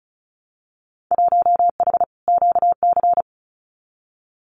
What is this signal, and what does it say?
Morse code, heard as a keyed tone, about 36 wpm, 700 Hz. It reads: 1H QC
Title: Winter Field Day Exchanges 35wpm